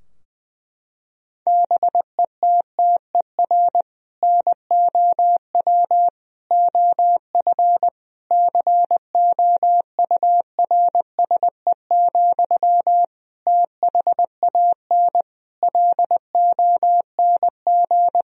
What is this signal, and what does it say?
Morse code, heard as a keyed tone, about 21 wpm, 700 Hz. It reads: BETTER NOW OF COURSE, THAN LONG
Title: Sentences from Top 1000 Words - Review Part 3 - 20wpm